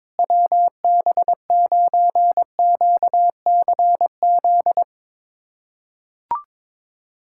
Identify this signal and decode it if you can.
Morse code, heard as a keyed tone, about 22 wpm, 700 Hz. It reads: WB9QCZ
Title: USA Callsign DE Callsign 22wpm